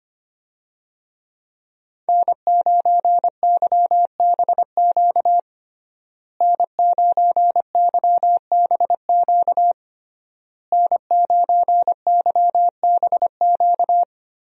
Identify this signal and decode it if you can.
Morse code, heard as a keyed tone, about 25 wpm, 700 Hz. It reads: N9YBQ N9YBQ N9YBQ
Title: USA Call Signs - 5 Characters - Repeated 3x - 25wpm